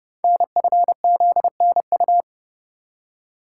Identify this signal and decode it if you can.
Morse code, heard as a keyed tone, about 28 wpm, 700 Hz. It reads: NFZNU
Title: Code Groups - 5 Letters - 30wpm